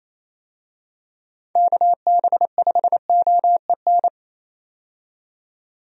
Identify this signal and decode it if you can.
Morse code, heard as a keyed tone, about 28 wpm, 700 Hz. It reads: KB5OEN
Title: USA Call Signs - 6 Characters - 28wpm